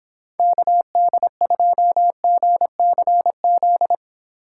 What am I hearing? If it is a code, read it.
Morse code, heard as a keyed tone, about 26 wpm, 700 Hz. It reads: KD2GCZ